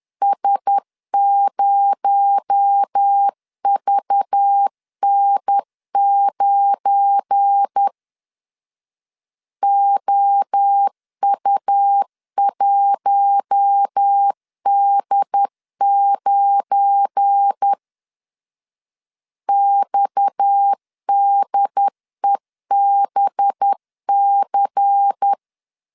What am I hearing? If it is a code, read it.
Morse code, heard as a keyed tone, about 10 wpm, 800 Hz. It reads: S0VN9 OU1D9 XDEBC
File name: LEZIONI / 7 GF9Z / 7 GF9 10wpm 2747